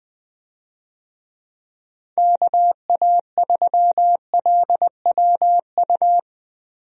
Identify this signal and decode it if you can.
Morse code, heard as a keyed tone, about 21 wpm, 700 Hz. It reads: KA3LWU
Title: USA Call Signs - 6 Characters - Repeated 2x - 20wpm